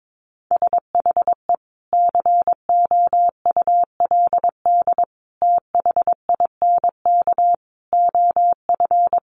Morse code, heard as a keyed tone, about 22 wpm, 700 Hz. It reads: SHE COULD THINK OF
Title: Sentences from Top 500 Words 22wpm